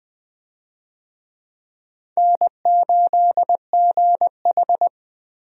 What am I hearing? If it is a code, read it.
Morse code, heard as a keyed tone, about 20 wpm, 700 Hz. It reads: N8GH
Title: USA Call Signs - 4 Characters - Repeated 3x - 20wpm